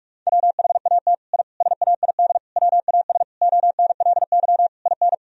Morse code, heard as a keyed tone, about 46 wpm, 700 Hz. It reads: WHAT I SAID WAS ONLY IN
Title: Sentences from Top 1000 Words - Review Part 4 - 45wpm